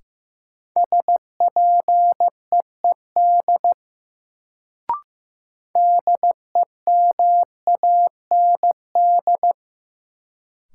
Morse code, heard as a keyed tone, about 16 wpm, 700 Hz. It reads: SPEED DEMAND
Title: Sentences from Top 800 Words - New Word Drill - 15wpm